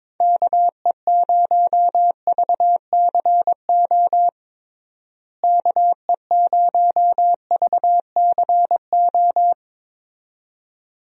Morse code, heard as a keyed tone, about 22 wpm, 700 Hz. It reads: KE0VCO KE0VCO
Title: USA Call Signs - 6 Characters - Repeated 2x - 22wpm